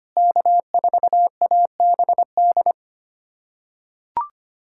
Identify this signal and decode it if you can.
Morse code, heard as a keyed tone, about 25 wpm, 700 Hz. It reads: K4ABD
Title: USA Callsign DE Callsign 25wpm